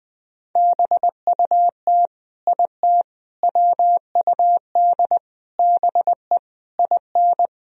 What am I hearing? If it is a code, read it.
Morse code, heard as a keyed tone, about 20 wpm, 700 Hz. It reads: BUT IT WUD BE IN
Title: Sentences from Top 300 Words and Abbreviations 20wpm